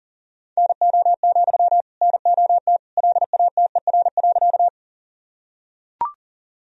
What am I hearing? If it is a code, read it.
Morse code, heard as a keyed tone, about 37 wpm, 700 Hz. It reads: NO, NOT LATER.
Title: Sentences from Top 1000 Words - Review Part 4 - 40wpm